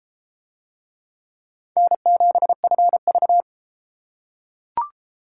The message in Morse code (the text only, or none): N7FV